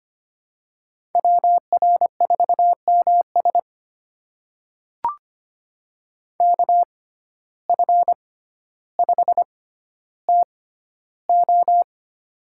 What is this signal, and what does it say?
Morse code, heard as a keyed tone, about 25 wpm, spaced as for 13 wpm, 700 Hz. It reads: WR4MS KF5TO
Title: USA Call Signs - 5 Characters - 6x Character Spacing - 25wpm